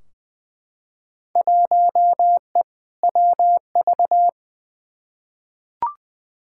Morse code, heard as a keyed tone, about 20 wpm, 700 Hz. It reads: 1E WV
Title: Field Day Exchanges Encore - 20wpm